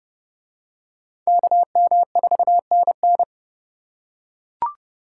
Morse code, heard as a keyed tone, about 26 wpm, 700 Hz. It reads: KM4NN